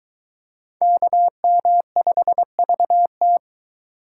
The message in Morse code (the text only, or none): KM5VT